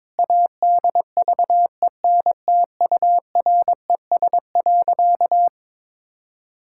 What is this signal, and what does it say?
Morse code, heard as a keyed tone, about 22 wpm, 700 Hz. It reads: ADVENTURES.
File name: Sentences with 7-Word Phrases - 22wpm